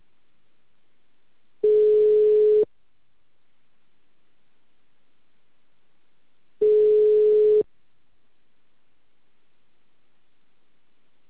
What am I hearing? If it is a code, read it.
Ringback tone (Europe, 425 Hz).
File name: out- -2044- - - .30481